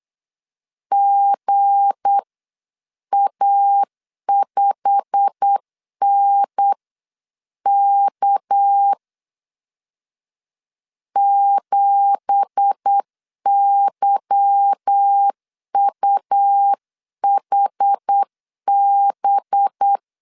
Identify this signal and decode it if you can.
Morse code, heard as a keyed tone, about 8 wpm, spaced as for 7 wpm, 800 Hz. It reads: GA5NK 7YUHB
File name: CW / DET / 1 9 1500C S A 8wpm 5637